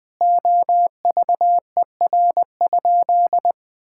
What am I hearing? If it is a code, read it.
Morse code, heard as a keyed tone, about 20 wpm, 700 Hz. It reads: OVER?
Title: Sentences with 4-Word Phrases - 20wpm